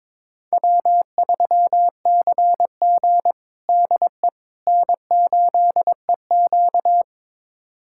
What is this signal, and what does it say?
Morse code, heard as a keyed tone, about 22 wpm, 700 Hz. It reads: W3CG DE N8EQ
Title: USA Callsign DE Callsign - 4 Characters - 22wpm